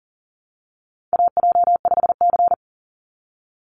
Morse code, heard as a keyed tone, about 40 wpm, 700 Hz. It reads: AJ5C
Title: NA SOTA Chaser Callsigns 40wpm